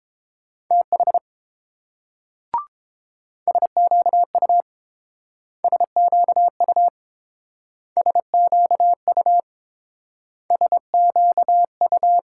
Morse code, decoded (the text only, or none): TH SQU SQU SQU SQU